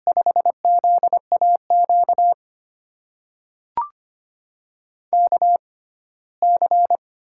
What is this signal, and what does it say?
Morse code, heard as a keyed tone, about 25 wpm, spaced as for 17 wpm, 700 Hz. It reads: 5ZAQ KC
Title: USA Call Signs - 6 Characters - 6x Character Spacing - 25wpm